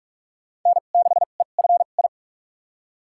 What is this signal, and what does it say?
Morse code, heard as a keyed tone, about 41 wpm, spaced as for 31 wpm, 700 Hz. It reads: N6EFI
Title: USA Call Signs - 5 Characters - 2x Character Spacing - 45wpm